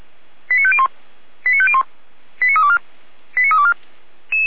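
Five-tone selective call (ZVEI), heard as quick runs of five tones, tones of about 70 ms. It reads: 89501 89501 89324 89324
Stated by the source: ZVEI five-tone selective call 89333 received on an SDR.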